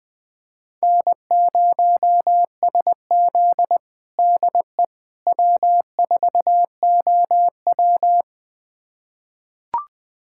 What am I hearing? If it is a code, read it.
Morse code, heard as a keyed tone, about 21 wpm, 700 Hz. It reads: N0SZ DE W4OW
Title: USA Callsign DE Callsign - 4 Characters - 20wpm